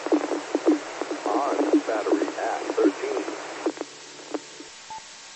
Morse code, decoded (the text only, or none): none